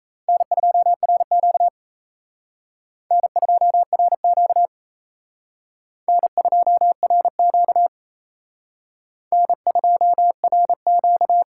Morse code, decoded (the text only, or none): N2RQ N2RQ N2RQ N2RQ